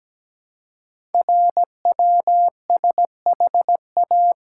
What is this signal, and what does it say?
Morse code, heard as a keyed tone, about 17 wpm, 700 Hz. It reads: RWSHA
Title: Code Groups - 5 Letters - 17wpm